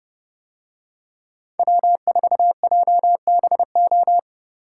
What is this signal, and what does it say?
Morse code, heard as a keyed tone, about 30 wpm, 700 Hz. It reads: W4JBO